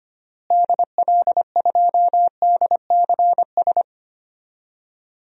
Morse code, heard as a keyed tone, about 25 wpm, 700 Hz. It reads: DL2DCS
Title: DXCC-100 Call Signs - 6 Characters - Repeated 3x - 25wpm